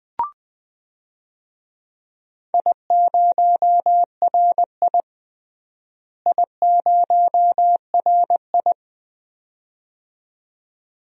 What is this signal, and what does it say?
Morse code, heard as a keyed tone, about 20 wpm, 700 Hz. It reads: I0RI I0RI
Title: DXCC-100 Call Signs - 4 Characters - Repeated 2x - 20wpm